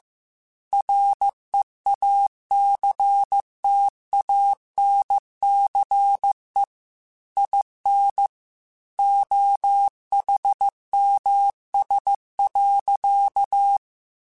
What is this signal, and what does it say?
Morse code, heard as a keyed tone, about 15 wpm, 800 Hz. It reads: REACTANCE IN OHMS.
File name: Technical Subjects / Technical Subjects-mp3s / XL @13